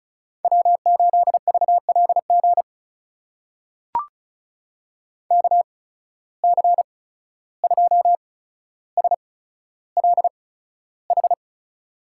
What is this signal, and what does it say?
Morse code, heard as a keyed tone, about 36 wpm, spaced as for 14 wpm, 700 Hz. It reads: W8VLG KC2SLH
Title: USA Call Signs - 8x Character Spacing - 35wpm